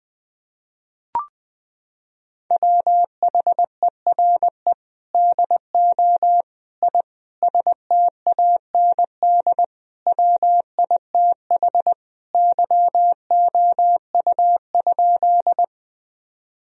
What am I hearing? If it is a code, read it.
Morse code, heard as a keyed tone, about 20 wpm, 700 Hz. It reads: WHERE DO I STAND WITH YOU?
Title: Sentences from Top 1000 Words - Review Part 1 - 20wpm